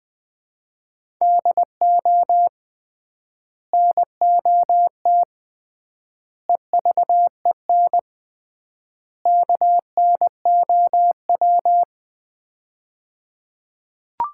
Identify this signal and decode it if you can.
Morse code, heard as a keyed tone, about 20 wpm, 700 Hz. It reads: DO NOT EVEN KNOW
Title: 3x Word-Spacing - Sets of 4 Words - 20wpm